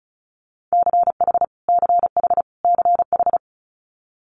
Morse code, heard as a keyed tone, about 34 wpm, 700 Hz. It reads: CH CH CH
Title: Multi-letter Phonemes Repeated 3x 35wpm